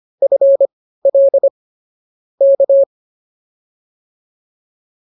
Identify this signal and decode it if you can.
Morse code, heard as a keyed tone, about 24 wpm, spaced as for 15 wpm, 550 Hz. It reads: FL K
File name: POTA203 15